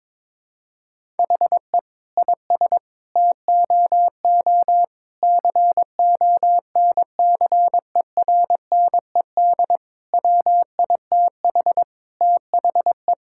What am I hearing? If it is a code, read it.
Morse code, heard as a keyed tone, about 22 wpm, 700 Hz. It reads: HE IS TOO CONCERNED WITH THE
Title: Sentences from Top 1000 Words - Review Part 6 - 22wpm